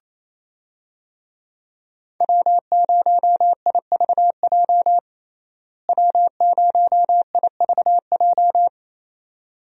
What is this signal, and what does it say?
Morse code, heard as a keyed tone, about 28 wpm, 700 Hz. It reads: W0IVJ W0IVJ
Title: USA Call Signs - 5 Characters - Repeated 2x - 28wpm